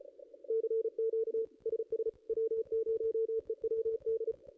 Morse code, heard as a keyed tone, about 31 wpm, 440 Hz. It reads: CQ SS W0EWD
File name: contests / ss-cw-k5tr-op-n5rz -0400